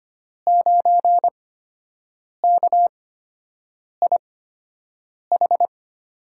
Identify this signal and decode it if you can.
Morse code, heard as a keyed tone, about 25 wpm, spaced as for 7 wpm, 700 Hz. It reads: 9KIH
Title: USA Call Signs - 5 Characters - 8x Character Spacing - 25wpm